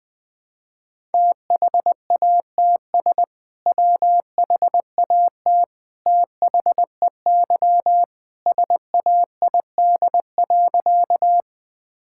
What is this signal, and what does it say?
Morse code, heard as a keyed tone, about 20 wpm, 700 Hz. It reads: THATS WHAT THEY SAID.
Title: Sentences with 2-Word Phrases - 20wpm